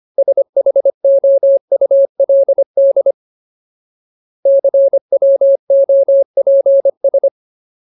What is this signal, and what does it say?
Morse code, heard as a keyed tone, about 25 wpm, 550 Hz. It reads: SHOULD CWOPS
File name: Ultimate Word List / 25WPM Ultimate